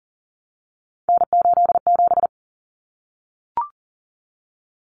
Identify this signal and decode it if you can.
Morse code, heard as a keyed tone, about 40 wpm, 700 Hz. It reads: N87
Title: Common Cut Numbers - 3-Digit 40wpm